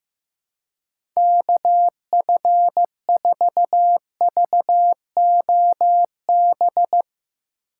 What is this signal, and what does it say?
Morse code, heard as a keyed tone, about 15 wpm, 700 Hz. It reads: KF4VOB
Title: USA Call Signs - 6 Characters - 4x Character Spacing - 15wpm